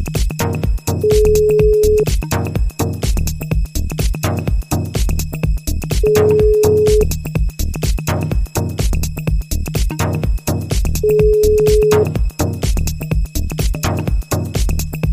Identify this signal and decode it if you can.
Ringback tone (Europe, 425 Hz).